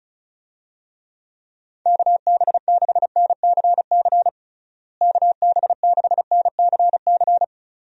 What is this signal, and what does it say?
Morse code, heard as a keyed tone, about 34 wpm, 700 Hz. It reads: KB6NCC KB6NCC
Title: USA Call Signs - 6 Characters - Repeated 2x - 35wpm